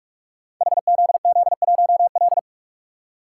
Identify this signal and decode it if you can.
Morse code, heard as a keyed tone, about 44 wpm, 700 Hz. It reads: HZZ1L
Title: Code Groups - 5 Letter:Numbers - 45wpm